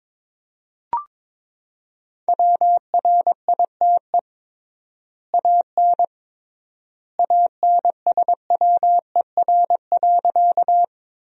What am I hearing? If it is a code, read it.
Morse code, heard as a keyed tone, about 22 wpm, 700 Hz. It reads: WRITE AN ANSWER.
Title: 3x Word-Spacing - Sentences from Top 300 Words - 22wpm